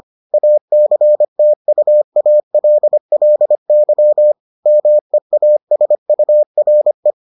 Morse code, heard as a keyed tone, about 25 wpm, 600 Hz. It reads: ACTUALLY MEASURE